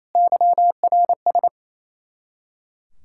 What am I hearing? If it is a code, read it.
Morse code, heard as a keyed tone, about 28 wpm, 700 Hz. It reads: YRS
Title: Practice QSO - Exchange 3 - More Info - 28wpm